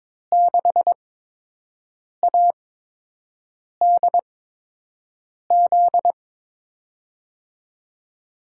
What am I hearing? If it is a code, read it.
Morse code, heard as a keyed tone, about 22 wpm, spaced as for 6 wpm, 700 Hz. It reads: 6ADZ
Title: USA Call Signs - 8x Character Spacing - 22wpm